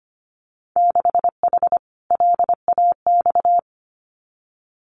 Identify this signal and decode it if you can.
Morse code, heard as a keyed tone, about 25 wpm, 700 Hz. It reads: 6H LAX
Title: Winter Field Day Exchanges Encore 25wpm